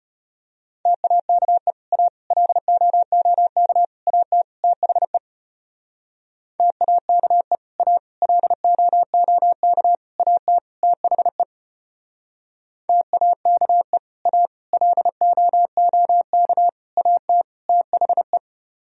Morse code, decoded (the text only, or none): TAKE A LOOK AT THE TAKE A LOOK AT THE TAKE A LOOK AT THE